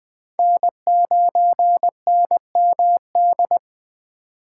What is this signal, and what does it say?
Morse code, heard as a keyed tone, about 21 wpm, 700 Hz. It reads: N9NMD
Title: Speed Racing - USA 1x3 Call Signs 17wpm